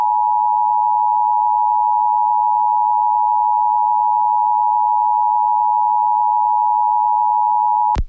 Emergency Alert System (SAME) tones (the 853 and 960 Hz attention tone). It is heard from 0.9 to 7.1 s.